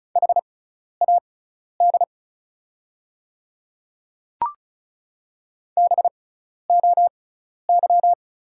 Morse code, heard as a keyed tone, about 36 wpm, spaced as for 12 wpm, 700 Hz. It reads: HAD BOY
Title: 6x Character Spacing - Top 200 Words - 35wpm